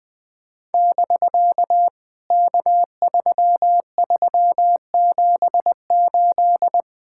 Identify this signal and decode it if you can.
Morse code, heard as a keyed tone, about 20 wpm, 700 Hz. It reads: <BK> K3378
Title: POTA QSO Practice - 20wpm